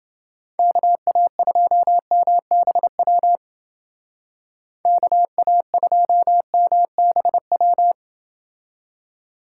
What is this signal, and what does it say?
Morse code, heard as a keyed tone, about 29 wpm, 700 Hz. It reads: KA2MBW KA2MBW